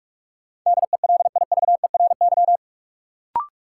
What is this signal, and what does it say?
Morse code, heard as a keyed tone, about 47 wpm, 700 Hz. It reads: DELIVERY
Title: Sentences from Top 1500 Words - New Word Drill Alt - 45wpm